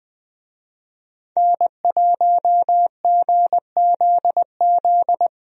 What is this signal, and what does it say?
Morse code, heard as a keyed tone, about 20 wpm, 700 Hz. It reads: N1GZZ
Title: USA Call Signs - 5 Characters - 20wpm